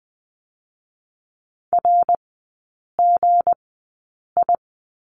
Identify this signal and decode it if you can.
Morse code, heard as a keyed tone, about 20 wpm, spaced as for 8 wpm, 700 Hz. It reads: RGI
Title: 184 Lesson 30e - Letter b - Three Characters 20wpm